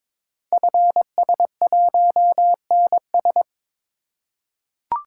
Morse code, heard as a keyed tone, about 22 wpm, 700 Hz. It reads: FS1NS
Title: Code Groups - 5 Letter:Numbers - 22wpm